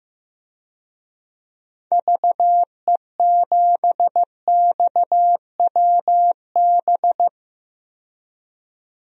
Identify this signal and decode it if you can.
Morse code, heard as a keyed tone, about 15 wpm, 700 Hz. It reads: VE7XWB
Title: DXCC-100 Call Signs - 6 Characters - 15wpm